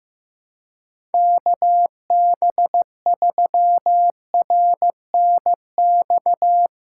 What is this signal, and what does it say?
Morse code, heard as a keyed tone, about 15 wpm, 700 Hz. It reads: KB3RNX
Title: USA Callsign DE Callsign 15wpm